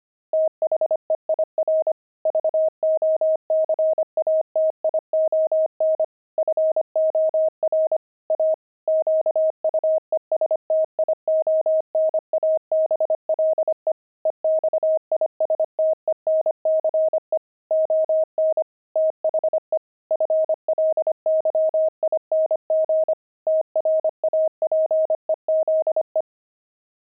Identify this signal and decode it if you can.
Morse code, heard as a keyed tone, about 25 wpm, 630 Hz. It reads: THEIR VOCATION FOR A QUESTIONABLE EXISTENCE ON THE FLYING TRAPEZE
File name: pangram-fs-25-cs-25-friars